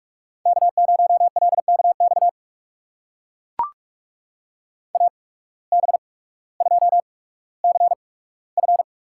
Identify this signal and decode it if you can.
Morse code, heard as a keyed tone, about 44 wpm, spaced as for 19 wpm, 700 Hz. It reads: K0LKX AB2CF
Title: USA Call Signs - 5 Characters - 8x Character Spacing - 45wpm